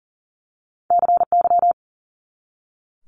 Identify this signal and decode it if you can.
Morse code, heard as a keyed tone, about 39 wpm, 700 Hz. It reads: CY
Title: Multi-letter Phonemes 40wpm